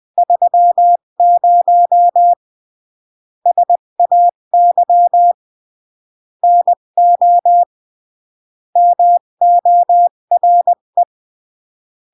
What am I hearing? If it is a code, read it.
Morse code, heard as a keyed tone, about 20 wpm, 700 Hz. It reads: 30 SAY NO MORE
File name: CW / jodorowsky / 82 commandments 20-15